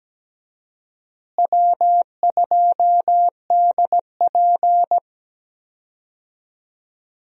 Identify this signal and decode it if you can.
Morse code, heard as a keyed tone, about 17 wpm, 700 Hz. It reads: W2DP
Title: USA Call Signs - 4 Characters - Repeated 3x - 17wpm